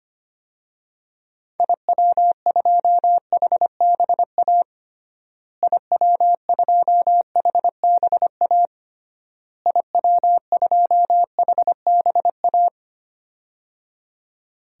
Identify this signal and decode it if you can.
Morse code, heard as a keyed tone, about 25 wpm, 700 Hz. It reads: IW2HBA IW2HBA IW2HBA
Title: DXCC-100 Call Signs - 6 Characters - Repeated 3x - 25wpm